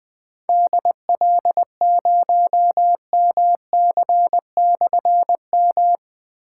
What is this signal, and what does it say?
Morse code, heard as a keyed tone, about 20 wpm, 700 Hz. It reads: DL0MC/M
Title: Difficult Call Signs 20wpm